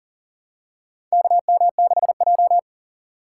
Morse code, heard as a keyed tone, about 37 wpm, 700 Hz. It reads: KM6J
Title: USA Call Signs - 4 Characters - 3x Character Spacing - 40wpm